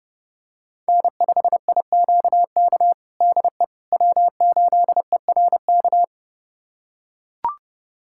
Morse code, heard as a keyed tone, about 30 wpm, 700 Hz. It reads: N5IQK DE W8ERK
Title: USA Callsign DE Callsign - 5 Characters - 30wpm